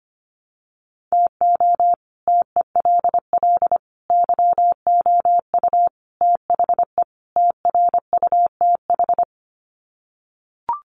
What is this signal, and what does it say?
Morse code, heard as a keyed tone, about 25 wpm, 700 Hz. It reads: TO TELL YOU THE TRUTH
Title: Sets of 5 Words 25wpm